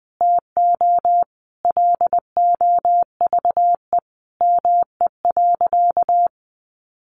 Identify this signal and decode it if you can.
Morse code, heard as a keyed tone, about 20 wpm, 700 Hz. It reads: TO LOVE ME.
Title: Sentences from Top 400 Words 20wpm